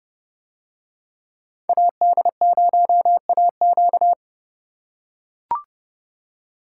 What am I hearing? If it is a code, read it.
Morse code, heard as a keyed tone, about 30 wpm, 700 Hz. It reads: AD0AQ